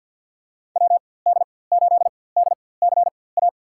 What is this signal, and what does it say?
Morse code, heard as a keyed tone, about 46 wpm, spaced as for 23 wpm, 700 Hz. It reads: WD8DCA
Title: USA Call Signs - 6 Characters - 4x Character Spacing - 50wpm